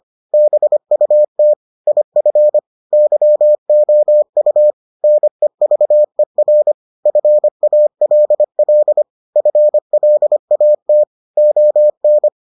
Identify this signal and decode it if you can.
Morse code, heard as a keyed tone, about 25 wpm, 600 Hz. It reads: BUT IF YOU NEVER FALL FLAT ON